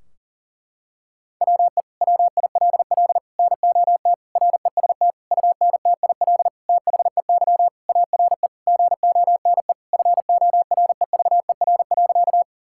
Morse code, heard as a keyed tone, about 40 wpm, 700 Hz. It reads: WE WILL NOT REST UNTIL THEY ARE GONE FOREVER.
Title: Sentences from Top 1600 Words 40wpm